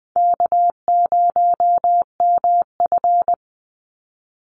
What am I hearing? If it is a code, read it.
Morse code, heard as a keyed tone, about 20 wpm, 700 Hz. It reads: K0MF
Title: USA Call Signs 20wpm